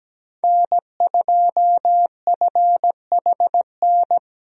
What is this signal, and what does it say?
Morse code, heard as a keyed tone, about 17 wpm, 700 Hz. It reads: N2FHN
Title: USA Call Signs - 5 Characters - Repeated 2x - 17wpm